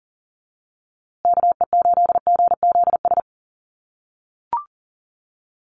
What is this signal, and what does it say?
Morse code, heard as a keyed tone, about 40 wpm, 700 Hz. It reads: KE8GZS
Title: USA Call Signs Repeated 3x 40wpm